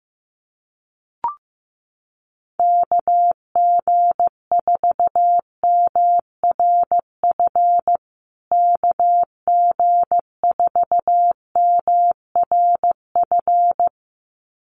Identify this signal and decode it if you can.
Morse code, heard as a keyed tone, about 15 wpm, 700 Hz. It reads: KG4MRF KG4MRF
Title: USA Call Signs Repeated 2x 15wpm